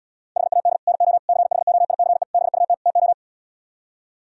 Morse code, heard as a keyed tone, about 77 wpm, 700 Hz. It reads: HAD NO CHOICE BUT TO